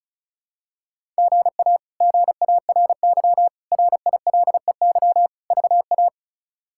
Morse code, heard as a keyed tone, about 36 wpm, 700 Hz. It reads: GA GARY RILEY VA
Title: K1USN SST QSO Practice - 35wpm